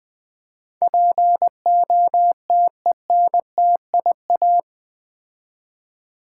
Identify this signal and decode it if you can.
Morse code, heard as a keyed tone, about 20 wpm, 700 Hz. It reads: POTENTIA
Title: Build A Word - 4 Letters At A Time - 11-Letter Words - 20wpm